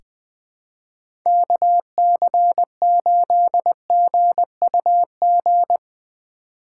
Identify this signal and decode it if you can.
Morse code, heard as a keyed tone, about 21 wpm, 700 Hz. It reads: KC8GUG
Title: USA Call Signs - 6 Characters - 8x Character Spacing - 20wpm